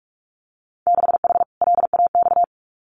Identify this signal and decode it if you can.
Morse code, heard as a keyed tone, about 46 wpm, 700 Hz. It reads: 6H LAX
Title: Winter Field Day Exchanges Encore 45wpm